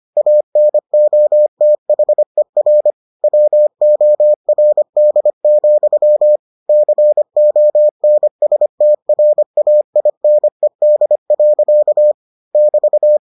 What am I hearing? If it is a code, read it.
Morse code, heard as a keyed tone, about 25 wpm, 600 Hz. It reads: ANOTHER WORD, CONSTRAINED. <BT>